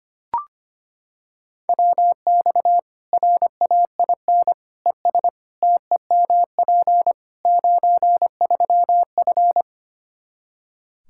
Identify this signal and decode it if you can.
Morse code, heard as a keyed tone, about 25 wpm, 700 Hz. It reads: WX RAIN ES TEMP 93F
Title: Practice QSO - Exchange 2 - Info - 25wpm